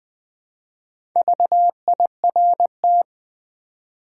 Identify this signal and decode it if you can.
Morse code, heard as a keyed tone, about 20 wpm, 700 Hz. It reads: VIRT
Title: Build A Word - 2 Letters At A Time - 6-Letter Words - 20wpm